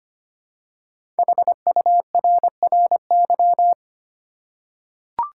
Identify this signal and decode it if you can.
Morse code, heard as a keyed tone, about 25 wpm, 700 Hz. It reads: HURRY
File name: 5 Letter Words - First 3 Letters Given - 25wpm